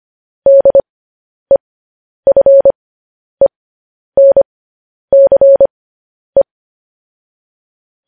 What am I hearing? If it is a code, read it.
Morse code, heard as a keyed tone, about 25 wpm, spaced as for 10 wpm, 550 Hz. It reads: DEFENCE
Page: news-headlines-in-morse-code-at-10-wpm